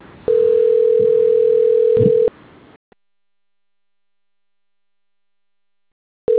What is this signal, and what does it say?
Ringback tone (North America), 440 + 480 Hz.